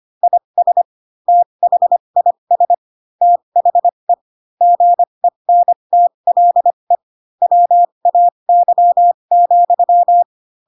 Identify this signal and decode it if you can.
Morse code, heard as a keyed tone, about 25 wpm, spaced as for 22 wpm, 700 Hz. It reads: IS THIS THE GENTLE WAY,
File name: Pinocchio / Pinocchio ch2 18 25 ik1hge